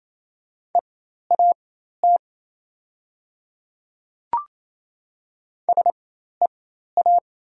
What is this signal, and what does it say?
Morse code, heard as a keyed tone, about 28 wpm, spaced as for 13 wpm, 700 Hz. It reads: EAT SEA
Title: 4x Character Spacing - Top 300 Words - 28wpm